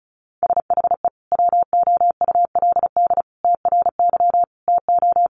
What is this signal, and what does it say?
Morse code, heard as a keyed tone, about 34 wpm, 700 Hz. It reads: SHE WOULD TRY TO
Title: Sentences from Top 600 Words 35wpm